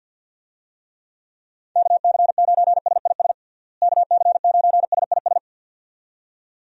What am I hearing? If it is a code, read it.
Morse code, heard as a keyed tone, about 47 wpm, 700 Hz. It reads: KC9SIS KC9SIS
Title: USA Call Signs - 6 Characters - Repeated 2x - 50wpm